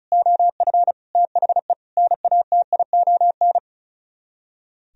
Morse code, heard as a keyed tone, about 35 wpm, 700 Hz. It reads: OF THE NATION
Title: Speed-Racing - Sets of 5 Words 35wpm